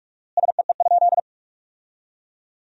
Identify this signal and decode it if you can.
Morse code, heard as a keyed tone, about 41 wpm, 700 Hz. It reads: SEE?
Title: 4x Word-Spacing - Sentences from Top 100 Words - 45wpm